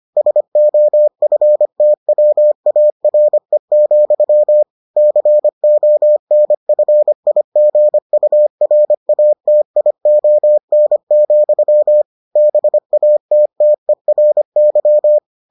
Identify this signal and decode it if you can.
Morse code, heard as a keyed tone, about 25 wpm, 600 Hz. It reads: SOFTWARE, CONFIGURATION, BATTERY